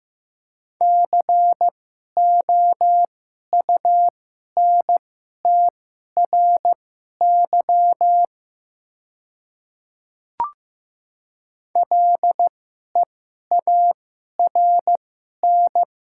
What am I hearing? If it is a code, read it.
Morse code, heard as a keyed tone, about 15 wpm, spaced as for 11 wpm, 700 Hz. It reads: COUNTRY LEARN
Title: 2x Character Spacing - Top 300 Words - 15wpm